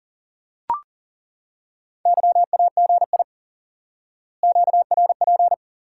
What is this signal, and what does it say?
Morse code, heard as a keyed tone, about 38 wpm, 700 Hz. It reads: YAGI QRP
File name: IQE Course 3 - Lesson 89 - YAGI - Two Elements - 40wpm